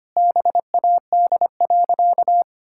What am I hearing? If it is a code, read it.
Morse code, heard as a keyed tone, about 25 wpm, 700 Hz. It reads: BAD.
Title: Sentences from Top 1000 Words - Review Part 5 - 25wpm